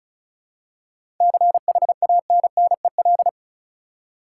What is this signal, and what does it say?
Morse code, heard as a keyed tone, about 36 wpm, 700 Hz. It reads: CHANNEL
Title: Seven Letter Words - 35wpm